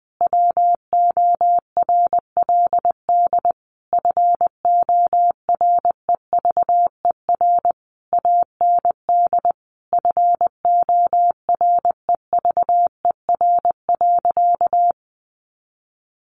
Morse code, heard as a keyed tone, about 20 wpm, 700 Hz. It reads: WORLD FOREVER AND FOREVER.
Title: A Sound Of Thunder 20wpm